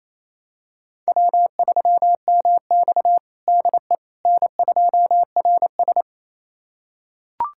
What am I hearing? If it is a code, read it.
Morse code, heard as a keyed tone, about 28 wpm, 700 Hz. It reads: W3MX DE N2RS
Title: USA Callsign DE Callsign - 4 Characters - 28wpm